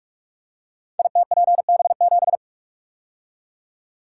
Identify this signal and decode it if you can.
Morse code, heard as a keyed tone, about 44 wpm, 700 Hz. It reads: ITPB7
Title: Code Groups - 5 Letter:Numbers - 45wpm